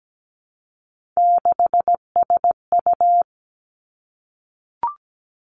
Morse code, heard as a keyed tone, about 17 wpm, 700 Hz. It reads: 6SU
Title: Three Challenging Letter-Number Combinations 17wpm